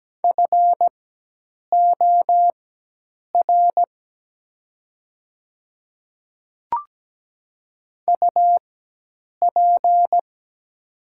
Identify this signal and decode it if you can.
Morse code, heard as a keyed tone, about 17 wpm, spaced as for 8 wpm, 700 Hz. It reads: FOR UP
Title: 4x Character Spacing - Top 100 Words - 17wpm